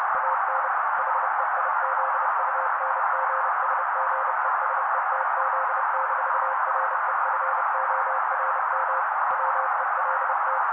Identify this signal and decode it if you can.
Morse code, heard as a keyed tone, about 30 wpm, 530 Hz. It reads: AN HEIZUNGSGESETZ-REFORM WELT
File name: daily / 29wpm noise